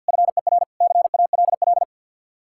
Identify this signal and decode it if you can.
Morse code, heard as a keyed tone, about 66 wpm, spaced as for 58 wpm, 700 Hz. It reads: FER CALL
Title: Speed-Racing Common QSO Phrases - 50wpm